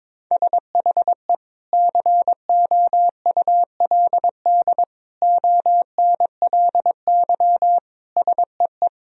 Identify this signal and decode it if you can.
Morse code, heard as a keyed tone, about 21 wpm, 700 Hz. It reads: SHE COULD ONLY SEE
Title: Sentences from Top 1000 Words - Review Part 2 - 22wpm